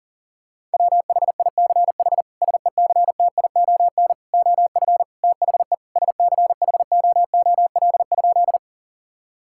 Morse code, heard as a keyed tone, about 40 wpm, 700 Hz. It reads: WHICH SECTION OF THE SCHOOL?
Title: Sentences from Top 1000 Words - Review Part 5 - 40wpm